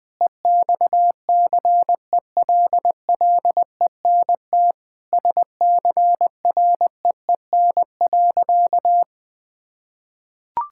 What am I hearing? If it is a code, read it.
Morse code, heard as a keyed tone, about 20 wpm, 700 Hz. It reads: EXCELLENT SCREEN.
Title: Sentences from Top 1000 Words - Review Part 6 - 20wpm